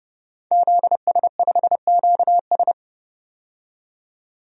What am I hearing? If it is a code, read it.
Morse code, heard as a keyed tone, about 31 wpm, 700 Hz. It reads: ZS5QS
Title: DXCC-100 Call Signs - 5 Characters - 30wpm